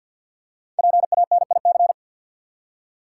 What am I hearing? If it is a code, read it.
Morse code, heard as a keyed tone, about 53 wpm, 700 Hz. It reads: PANIC
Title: Five Letter Words - 50wpm